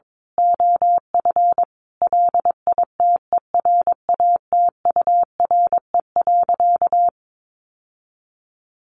Morse code, heard as a keyed tone, about 22 wpm, 700 Hz. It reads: OF LITERATURE.
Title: Sentences from Top 900 Words 22wpm